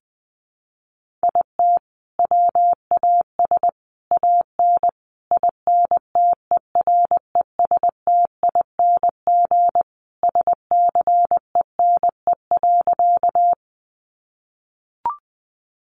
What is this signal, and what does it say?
Morse code, heard as a keyed tone, about 20 wpm, 700 Hz. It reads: IT WAS AN INTERESTING SCENE.
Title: Sentences from Top 1100 Words 20wpm